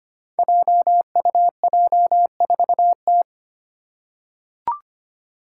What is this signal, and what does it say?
Morse code, heard as a keyed tone, about 25 wpm, 700 Hz. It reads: JUJ4T
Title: Five Challenging Letter-Number Combinations 25wpm